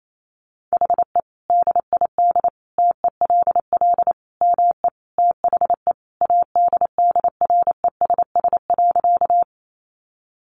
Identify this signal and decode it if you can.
Morse code, heard as a keyed tone, about 28 wpm, 700 Hz. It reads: HE DID TELL ME THE ADDRESS.
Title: Sentences from Top 1000 Words 28wpm